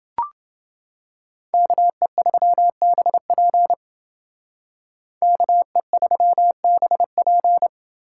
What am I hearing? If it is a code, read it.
Morse code, heard as a keyed tone, about 28 wpm, 700 Hz. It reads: KE3BP KE3BP